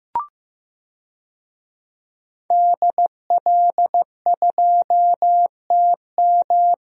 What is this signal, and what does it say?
Morse code, heard as a keyed tone, about 15 wpm, 700 Hz. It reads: DL2TM
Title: DXCC-100 Call Signs - 5 Characters - Repeated 3x - 15wpm